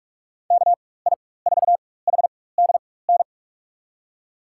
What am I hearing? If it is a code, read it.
Morse code, heard as a keyed tone, about 43 wpm, spaced as for 21 wpm, 700 Hz. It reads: KI4HDN
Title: USA Call Signs - 6 Characters - 4x Character Spacing - 45wpm